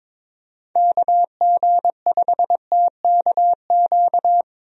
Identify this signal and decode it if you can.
Morse code, heard as a keyed tone, about 22 wpm, 700 Hz. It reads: KG5TKQ
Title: USA Call Signs - 6 Characters - Repeated 2x - 22wpm